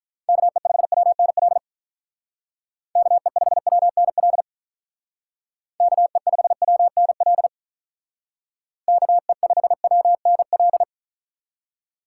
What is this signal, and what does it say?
Morse code, heard as a keyed tone, about 46 wpm, 700 Hz. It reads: KE5WNL KE5WNL KE5WNL KE5WNL